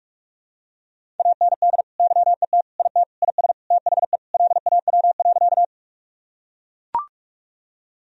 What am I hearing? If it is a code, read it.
Morse code, heard as a keyed tone, about 43 wpm, 700 Hz. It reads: AND YET IT IS THE LAW.
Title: Sentences from Top 1000 Words - Review Part 3 - 45wpm